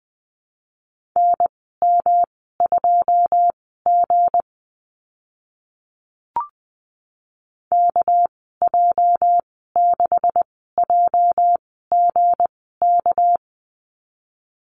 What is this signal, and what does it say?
Morse code, heard as a keyed tone, about 20 wpm, spaced as for 14 wpm, 700 Hz. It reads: NM2G KJ6JGK
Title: 207 Lesson 33g - Letter j - Call Signs 20wpm